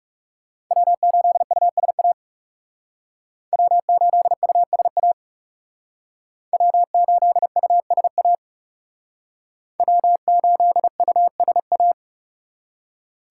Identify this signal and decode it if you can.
Morse code, heard as a keyed tone, about 38 wpm, 700 Hz. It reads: W8USA W8USA W8USA W8USA